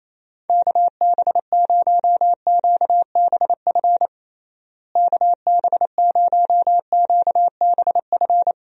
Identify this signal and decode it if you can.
Morse code, heard as a keyed tone, about 28 wpm, 700 Hz. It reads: KB0QBF KB0QBF
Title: USA Call Signs - 6 Characters - Repeated 2x - 28wpm